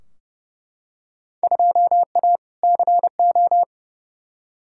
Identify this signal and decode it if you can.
Morse code, heard as a keyed tone, about 30 wpm, 700 Hz. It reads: 2A CO
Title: Field Day Exchanges - 30wpm